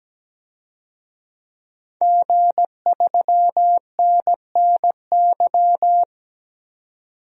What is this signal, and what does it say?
Morse code, heard as a keyed tone, about 17 wpm, 700 Hz. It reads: G3NNY
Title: DXCC-100 Call Signs - 5 Characters - 17wpm